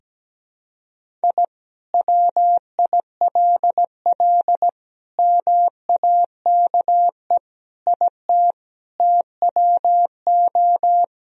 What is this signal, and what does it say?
Morse code, heard as a keyed tone, about 17 wpm, 700 Hz. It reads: I WILL MAKE IT TWO
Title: Sentences from Top 1000 Words - Review Part 2 - 17wpm